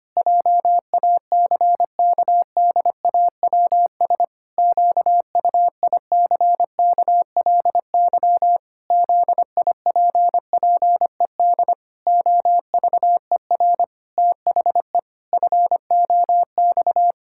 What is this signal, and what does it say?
Morse code, heard as a keyed tone, about 25 wpm, 700 Hz. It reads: JACKDAWS QUICKLY ZIPPED OVER THE FOX
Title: Pangram Challenge v2 - 25wpm